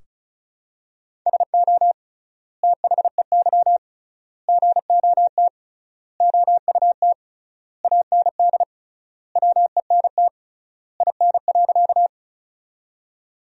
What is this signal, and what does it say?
Morse code, heard as a keyed tone, about 34 wpm, 700 Hz. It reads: SO THEY GOT OUT AND WENT IN.
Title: 3x Word-Spacing - Sentences from Top 300 Words - 35wpm